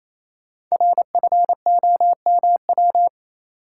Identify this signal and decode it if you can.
Morse code, heard as a keyed tone, about 29 wpm, 700 Hz. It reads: RFOMW
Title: Code Groups - 5 Letters - 28wpm